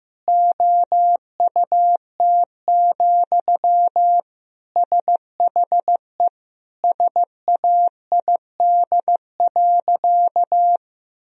Morse code, heard as a keyed tone, about 15 wpm, 700 Hz. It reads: OUT, SHE SAID.
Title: Sentences from Top 1000 Words - Review Part 3 - 15wpm